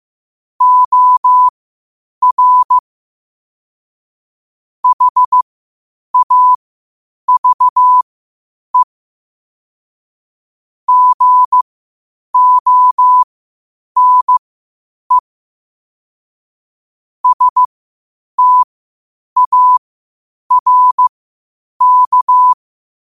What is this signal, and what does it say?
Morse code, heard as a keyed tone, about 15 wpm, spaced as for 8 wpm, 1000 Hz. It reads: OR HAVE GONE STARK
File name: qotdfiles / 8wpm